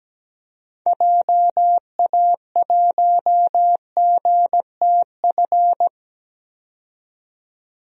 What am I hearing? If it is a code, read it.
Morse code, heard as a keyed tone, about 17 wpm, 700 Hz. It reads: JA1GTF
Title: DXCC-100 Call Signs - 6 Characters - 17wpm